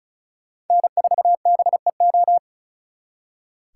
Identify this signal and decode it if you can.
Morse code, heard as a keyed tone, about 45 wpm, 700 Hz. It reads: N4BEO